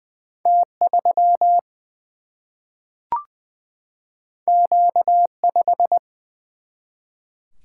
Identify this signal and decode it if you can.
Morse code, heard as a keyed tone, about 20 wpm, 700 Hz. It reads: T3 Q5
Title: Single-Letter Single-Number 20wpm